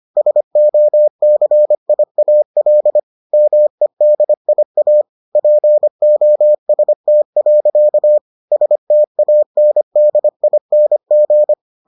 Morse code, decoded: SOCIAL MEDIA POST. STANDING

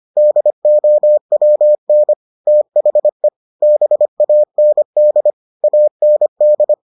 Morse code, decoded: DOWN THE BAND AND

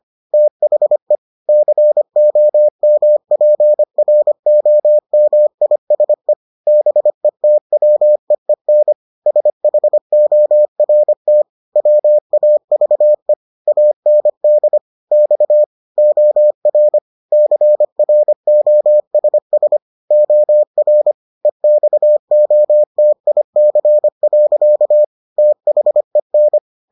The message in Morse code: THE COMPROMISE BETWEEN SHORT WAVE AND X OR CROSS OR EXOTIC. THEN

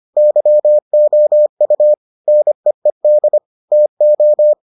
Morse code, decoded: YOU NEED TO

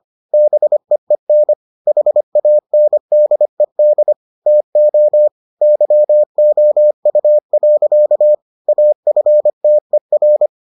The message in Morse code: BEEN HANDED TO YOU. AFTER

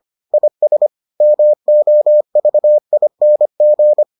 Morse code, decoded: IS MOVING

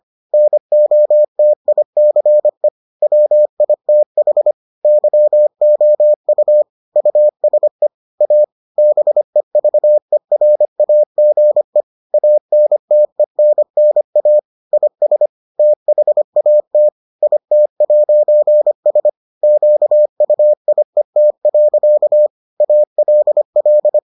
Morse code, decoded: NOTICE WITH YOU USE A BEVERAGE ANTENNA IS THAT IT'S QUIET. ALL

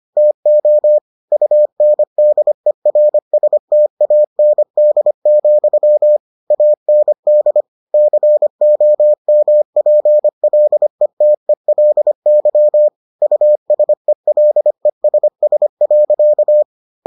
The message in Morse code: TO UNDERSTAND, AND COMPLETELY USELESS.

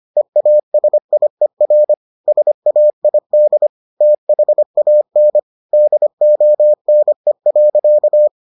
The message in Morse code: EASIER SAID THAN DONE.

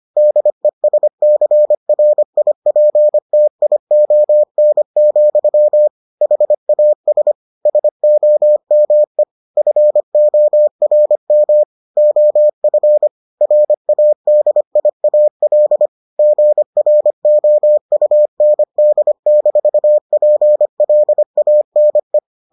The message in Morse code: DESCRIPTION, HAS SOME FORM OF RADIAL GROUND-PLANE

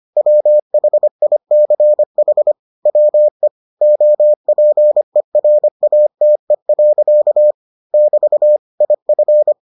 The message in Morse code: WHICH WE OPERATE. <BT> IF

